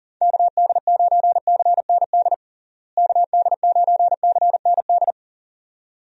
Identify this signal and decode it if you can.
Morse code, heard as a keyed tone, about 40 wpm, 700 Hz. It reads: KD9CND KD9CND
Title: USA Call Signs - 6 Characters - Repeated 2x - 40wpm